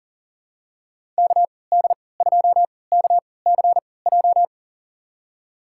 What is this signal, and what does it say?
Morse code, heard as a keyed tone, about 40 wpm, spaced as for 23 wpm, 700 Hz. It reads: KD2KCJ
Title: USA Call Signs - 6 Characters - 3x Character Spacing - 40wpm